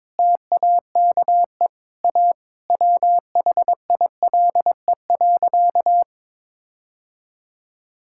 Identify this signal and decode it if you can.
Morse code, heard as a keyed tone, about 22 wpm, 700 Hz. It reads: TAKE A WHILE.
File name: practice / 22wpm / Sentences with 4-Word Phrases - 22wpm